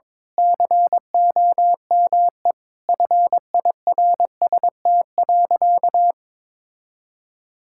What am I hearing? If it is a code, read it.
Morse code, heard as a keyed tone, about 22 wpm, 700 Hz. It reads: COME FIRST.
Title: Sentences from Top 1000 Words - Review Part 6 - 22wpm